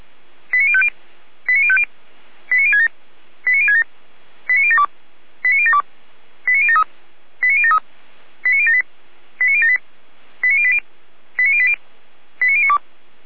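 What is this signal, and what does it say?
Five-tone selective call (ZVEI), heard as quick runs of five tones, tones of about 80 ms. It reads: 89050 89050 89067 89067 89072 89072 89073 89073 89078 89078 89080 89080 89092
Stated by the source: ZVEI five-tone selective call received by SDR.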